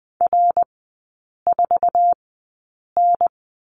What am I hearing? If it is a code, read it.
Morse code, heard as a keyed tone, about 20 wpm, spaced as for 8 wpm, 700 Hz. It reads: R4N
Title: 060 Lesson 11e - Letter h - Three Characters 20wpm